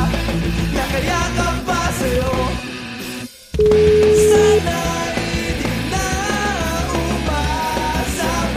Ringback tone (Europe, 425 Hz), with noise.